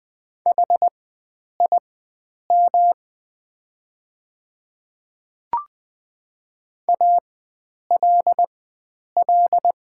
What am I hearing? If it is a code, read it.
Morse code, heard as a keyed tone, about 20 wpm, spaced as for 9 wpm, 700 Hz. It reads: HIM ALL
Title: 4x Character Spacing - Top 100 Words - 20wpm